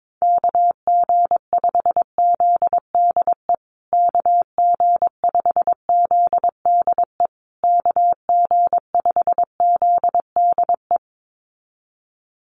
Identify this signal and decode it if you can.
Morse code, heard as a keyed tone, about 22 wpm, 700 Hz. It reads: KG5ZDE KG5ZDE KG5ZDE
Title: USA Call Signs Repeated 3x 22wpm